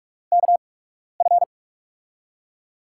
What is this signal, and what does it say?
Morse code, heard as a keyed tone, about 43 wpm, spaced as for 12 wpm, 700 Hz. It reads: KF
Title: USA Call Signs - 6 Characters - 8x Character Spacing - 45wpm